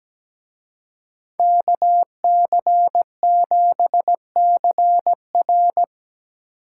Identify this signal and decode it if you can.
Morse code, heard as a keyed tone, about 20 wpm, 700 Hz. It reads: KC7CR
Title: Speed Racing - USA 2x2 Call Signs 17wpm